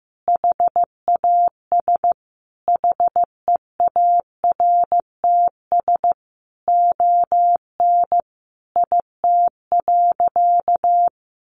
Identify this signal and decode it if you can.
Morse code, heard as a keyed tone, about 15 wpm, 700 Hz. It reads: HAS HEARTS ON IT.
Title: Additional Easy Phrases 15wpm